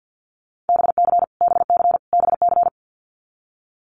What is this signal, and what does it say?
Morse code, heard as a keyed tone, about 50 wpm, 700 Hz. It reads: BC BC BC
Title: Canadian Provinces and Territories - Repeated 3x - 50wpm